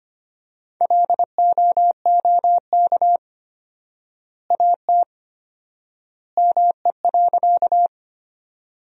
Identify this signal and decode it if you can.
Morse code, heard as a keyed tone, about 25 wpm, 700 Hz. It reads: LOOK AT ME.
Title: 4x Word-Spacing - Sentences from Top 200 Words - 25wpm